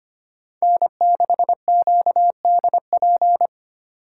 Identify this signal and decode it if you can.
Morse code, heard as a keyed tone, about 33 wpm, 700 Hz. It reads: N6QDP